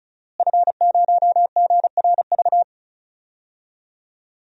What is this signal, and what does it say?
Morse code, heard as a keyed tone, about 36 wpm, 700 Hz. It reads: F0GRV
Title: DXCC-100 Call Signs - 5 Characters - Repeated 2x - 35wpm